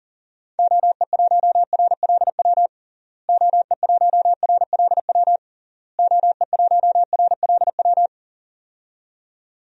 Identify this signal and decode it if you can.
Morse code, heard as a keyed tone, about 37 wpm, 700 Hz. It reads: OE1RLW OE1RLW OE1RLW
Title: DXCC-100 Call Signs - 6 Characters - Repeated 3x - 40wpm